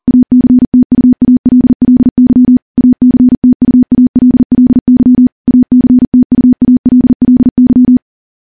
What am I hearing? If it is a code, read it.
Morse code, heard as a keyed tone, about 40 wpm, 250 Hz. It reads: ACTUALLY ACTUALLY ACTUALLY
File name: morse-training-40WPM